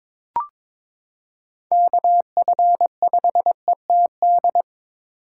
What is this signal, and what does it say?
Morse code, heard as a keyed tone, about 22 wpm, 700 Hz. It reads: KF5ETD
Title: K1USN SST QSO Practice - 22wpm